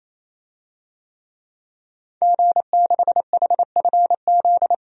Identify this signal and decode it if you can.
Morse code, heard as a keyed tone, about 28 wpm, 700 Hz. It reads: G6HFZ
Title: DXCC-100 Call Signs - 5 Characters - 28wpm